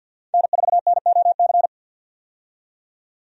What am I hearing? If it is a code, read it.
Morse code, heard as a keyed tone, about 52 wpm, 700 Hz. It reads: N4NOC
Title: USA Call Signs - 5 Characters - 50wpm